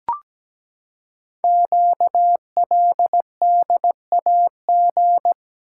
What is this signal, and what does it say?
Morse code, heard as a keyed tone, about 17 wpm, 700 Hz. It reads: QLDAG
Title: Code Groups - 5 Letters - 17wpm